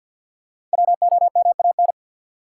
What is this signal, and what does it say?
Morse code, heard as a keyed tone, about 47 wpm, 700 Hz. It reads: WOMAN
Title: Sentences from Top 1000 Words - Review Part 2 - New Word Drill - 50wpm